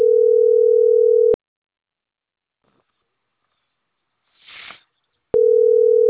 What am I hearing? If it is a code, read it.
Ringback tone (North America), 440 + 480 Hz.